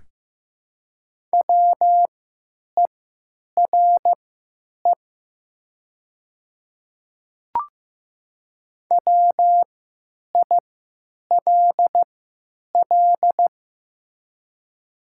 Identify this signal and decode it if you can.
Morse code, heard as a keyed tone, about 15 wpm, spaced as for 9 wpm, 700 Hz. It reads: WERE WILL
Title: 3x Character Spacing - Top 100 Words - 15wpm